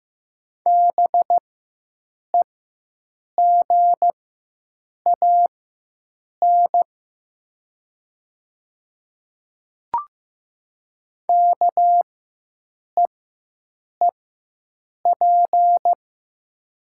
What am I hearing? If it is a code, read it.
Morse code, heard as a keyed tone, about 15 wpm, spaced as for 7 wpm, 700 Hz. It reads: BEGAN KEEP
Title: 4x Character Spacing - Top 300 Words - 15wpm